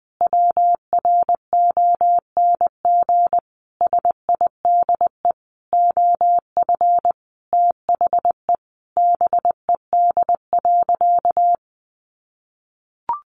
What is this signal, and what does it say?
Morse code, heard as a keyed tone, about 20 wpm, 700 Hz. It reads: WRONG SIDE OF THE BED.
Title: Sentences from Top 900 Words 20wpm